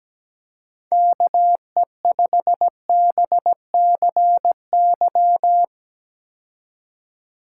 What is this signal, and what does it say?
Morse code, heard as a keyed tone, about 17 wpm, 700 Hz. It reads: KE5BCY
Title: K1USN SST QSO Practice - 17wpm